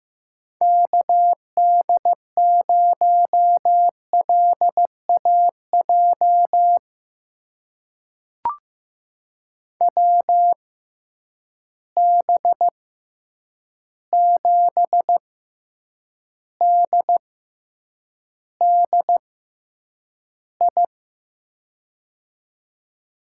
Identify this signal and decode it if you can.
Morse code, heard as a keyed tone, about 15 wpm, spaced as for 8 wpm, 700 Hz. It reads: KD0LAJ WB7DDI
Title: USA Call Signs - 6 Characters - 6x Character Spacing - 15wpm